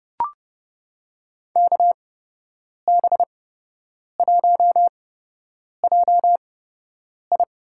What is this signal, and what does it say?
Morse code, heard as a keyed tone, about 30 wpm, spaced as for 8 wpm, 700 Hz. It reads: KB1JI
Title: USA Call Signs - 8x Character Spacing - 30wpm